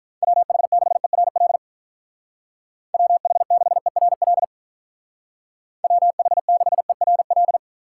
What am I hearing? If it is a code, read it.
Morse code, heard as a keyed tone, about 43 wpm, 700 Hz. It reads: WH6ERL WH6ERL WH6ERL
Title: Speed Racing - USA 2x3 Call Signs 35wpm